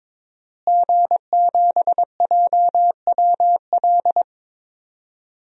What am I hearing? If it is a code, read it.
Morse code, heard as a keyed tone, about 23 wpm, 700 Hz. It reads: G7JWL
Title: DXCC-100 Call Signs - 5 Characters - Repeated 3x - 22wpm